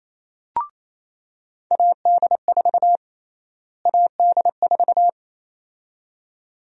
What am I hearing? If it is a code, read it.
Morse code, heard as a keyed tone, about 28 wpm, 700 Hz. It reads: AD4 AD4
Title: USA Call Sign Prefixes - Repeated 2x - 28wpm